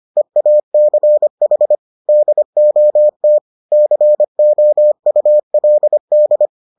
Morse code, heard as a keyed tone, about 25 wpm, 600 Hz. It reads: EACH DOT COULD